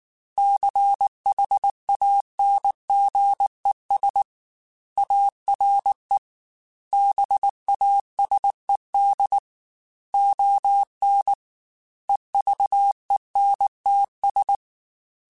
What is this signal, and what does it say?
Morse code, heard as a keyed tone, about 19 wpm, 800 Hz. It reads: CHANGES ARE BASED ON EVENTS
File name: Technical Subjects / Technical Subjects-mp3s / Reactance @18